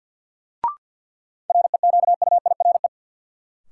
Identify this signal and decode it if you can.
Morse code, heard as a keyed tone, about 52 wpm, 700 Hz. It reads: REQUIRE
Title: Sentences from Top 900 Words - New Word Drill - 50wpm